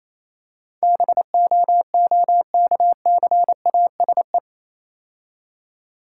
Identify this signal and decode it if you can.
Morse code, heard as a keyed tone, about 27 wpm, 700 Hz. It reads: BOOKCASE
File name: Easy Words - Speed-Racing with Hints - 22wpm